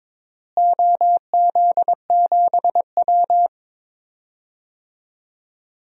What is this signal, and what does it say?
Morse code, heard as a keyed tone, about 22 wpm, 700 Hz. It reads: OZ7W
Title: DXCC-100 Call Signs - 4 Characters - Repeated 2x - 22wpm